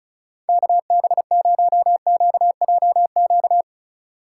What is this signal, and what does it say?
Morse code, heard as a keyed tone, about 36 wpm, 700 Hz. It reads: KB0QJQ
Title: USA Call Signs - 3x Character Spacing - 35wpm